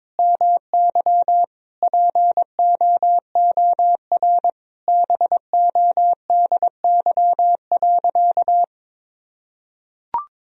Morse code, heard as a keyed tone, about 22 wpm, 700 Hz. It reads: MY POOR BODY.
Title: Sentences from Top 1000 Words - Review Part 2 - 22wpm